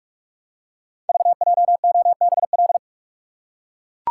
Morse code, heard as a keyed tone, about 44 wpm, 690 Hz. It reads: VJOBL
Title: Code Groups - 5 Letters - 45wpm